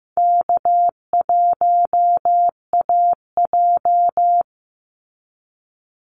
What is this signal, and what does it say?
Morse code, heard as a keyed tone, about 15 wpm, 700 Hz. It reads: K1AJ
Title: Call Signs Repeated 3x - 15wpm